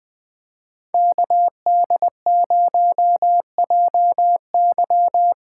Morse code, heard as a keyed tone, about 21 wpm, 700 Hz. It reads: KD0JY
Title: USA Call Signs - 8x Character Spacing - 20wpm